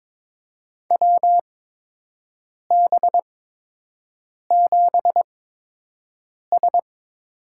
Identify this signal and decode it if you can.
Morse code, heard as a keyed tone, about 22 wpm, spaced as for 6 wpm, 700 Hz. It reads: WB7S
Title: USA Call Signs - 8x Character Spacing - 22wpm